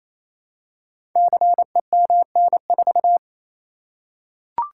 Morse code, heard as a keyed tone, about 28 wpm, 700 Hz. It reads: CEMN4
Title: Code Groups - 5 Letter:Numbers - 28wpm